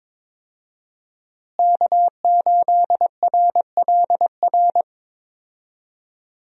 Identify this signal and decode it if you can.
Morse code, heard as a keyed tone, about 22 wpm, 700 Hz. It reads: K8RLR
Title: USA Call Signs - 5 Characters - Repeated 3x - 22wpm